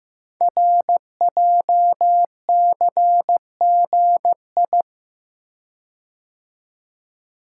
Morse code, heard as a keyed tone, about 15 wpm, 700 Hz. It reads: RJCGI
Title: Code Groups - 5 Letters - 15wpm